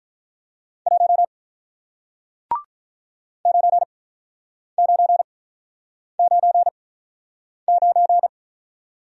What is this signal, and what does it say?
Morse code, heard as a keyed tone, about 48 wpm, 700 Hz. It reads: 1 9 9 9 9